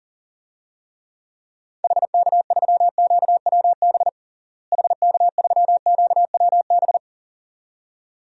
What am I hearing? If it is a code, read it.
Morse code, heard as a keyed tone, about 39 wpm, 700 Hz. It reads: HK3QWB HK3QWB
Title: DXCC-100 Call Signs - 6 Characters - Repeated 2x - 40wpm